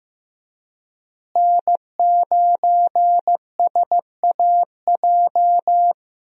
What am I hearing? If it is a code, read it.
Morse code, heard as a keyed tone, about 14 wpm, 700 Hz. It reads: N9SAJ
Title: USA Call Signs - 5 Characters - Repeated 3x - 15wpm